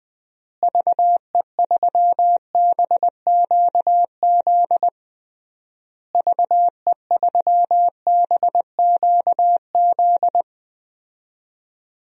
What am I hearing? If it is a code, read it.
Morse code, heard as a keyed tone, about 20 wpm, 700 Hz. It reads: VE3BQZ VE3BQZ
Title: DXCC-100 Call Signs - 6 Characters - Repeated 3x - 20wpm